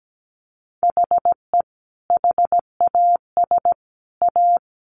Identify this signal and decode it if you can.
Morse code, heard as a keyed tone, about 17 wpm, 700 Hz. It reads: HE HAS A
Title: Sentences from Top 500 Words 17wpm